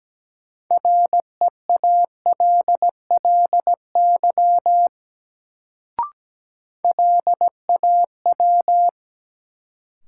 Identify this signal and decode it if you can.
Morse code, heard as a keyed tone, about 17 wpm, 700 Hz. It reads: REALLY LAW
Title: Sentences from Top 600 Words - New Word Drill - 17wpm